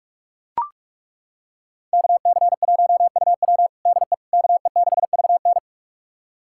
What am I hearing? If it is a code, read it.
Morse code, heard as a keyed tone, about 42 wpm, 700 Hz. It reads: KC1UW DE KE6VN
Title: USA Callsign DE Callsign - 5 Characters - 45wpm